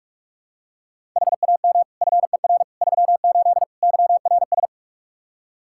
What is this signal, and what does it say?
Morse code, heard as a keyed tone, about 46 wpm, 700 Hz. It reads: HAM FER 38 YRS
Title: K1USN SST QSO Practice 45wpm